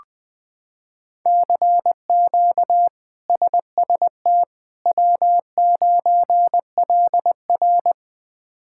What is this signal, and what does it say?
Morse code, heard as a keyed tone, about 20 wpm, 700 Hz. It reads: CQ SST W9LR
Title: K1USN SST QSO Practice - 20wpm